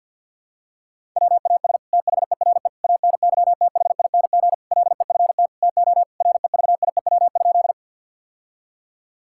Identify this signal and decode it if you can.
Morse code, heard as a keyed tone, about 50 wpm, 700 Hz. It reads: WAS THERE ANYTHING LEFT TO REVIEW?
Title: Sentences from Top 1000 Words - Review Part 6 - 50wpm